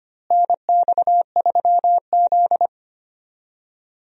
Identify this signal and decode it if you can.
Morse code, heard as a keyed tone, about 25 wpm, 700 Hz. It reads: NX3Z
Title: USA Call Signs - 4 Characters - Repeated 3x - 25wpm